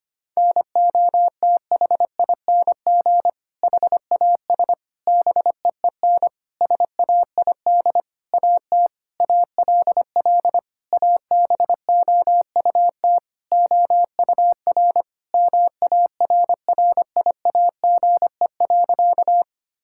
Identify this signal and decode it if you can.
Morse code, heard as a keyed tone, about 25 wpm, 700 Hz. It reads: NOTHING HAS BEEN SAID AT ALL ABOUT OUR MARRIAGE.
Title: Sentences from Top 1000 Words - Review Part 5 - 25wpm